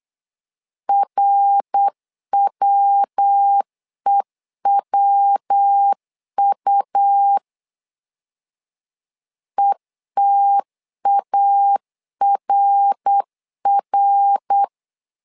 Morse code, heard as a keyed tone, about 8 wpm, 800 Hz. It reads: RWEWU ETARR
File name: LEZIONI / 3 MWRO / 3 MWR 8wpm 2444